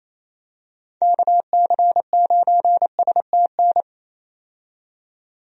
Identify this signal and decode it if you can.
Morse code, heard as a keyed tone, about 28 wpm, 700 Hz. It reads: KC9STN
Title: USA Call Signs - 6 Characters - 28wpm